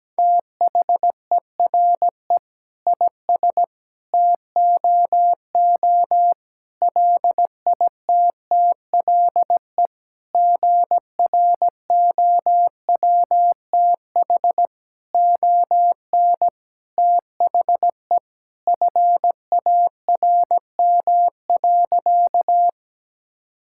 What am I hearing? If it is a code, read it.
Morse code, heard as a keyed tone, about 17 wpm, 700 Hz. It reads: THERE IS TOO LITTLE GROWTH ON THE FARM.
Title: Sentences from Top 1000 Words - Review Part 5 - 17wpm